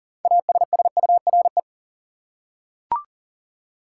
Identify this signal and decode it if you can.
Morse code, heard as a keyed tone, about 40 wpm, 700 Hz. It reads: ASSURE
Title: Six Letter Words - 40wpm